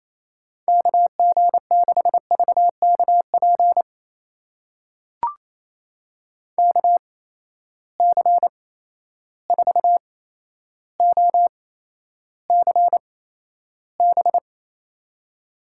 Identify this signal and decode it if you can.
Morse code, heard as a keyed tone, about 28 wpm, spaced as for 12 wpm, 700 Hz. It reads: KG6VKP KC4OCB
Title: USA Call Signs - 6 Characters - 8x Character Spacing - 28wpm